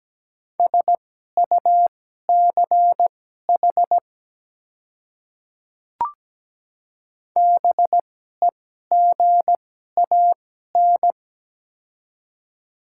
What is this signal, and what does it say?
Morse code, heard as a keyed tone, about 17 wpm, spaced as for 12 wpm, 700 Hz. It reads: SUCH BEGAN
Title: 2x Character Spacing - Top 300 Words - 17wpm